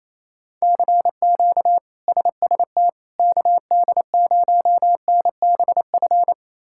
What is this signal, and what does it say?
Morse code, heard as a keyed tone, about 27 wpm, 700 Hz. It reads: CQ SST KD0NBF
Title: K1USN SST QSO Practice - 28wpm